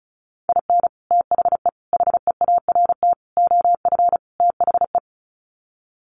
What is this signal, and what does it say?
Morse code, heard as a keyed tone, about 34 wpm, 700 Hz. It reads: IN THE HEART OF THE
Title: Sets of 5 Words 35wpm